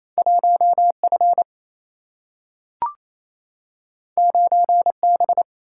Morse code, heard as a keyed tone, about 28 wpm, 700 Hz. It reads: 1F 9B
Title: Single-Letter Single-Number 28wpm